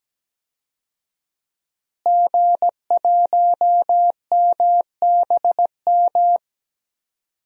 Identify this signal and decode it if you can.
Morse code, heard as a keyed tone, about 17 wpm, 700 Hz. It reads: G1MBM
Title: DXCC-100 Call Signs - 5 Characters - Repeated 2x - 17wpm